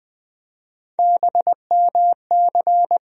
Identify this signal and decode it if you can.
Morse code, heard as a keyed tone, about 20 wpm, 700 Hz. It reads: BMC
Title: USA Call Sign Postfixes - 3 Letters 20wpm